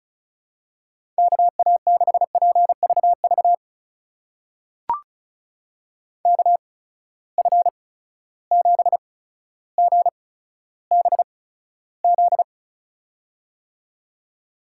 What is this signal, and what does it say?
Morse code, heard as a keyed tone, about 36 wpm, spaced as for 15 wpm, 700 Hz. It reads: KA6PVV KF7GBZ
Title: USA Call Signs - 6 Characters - 8x Character Spacing - 35wpm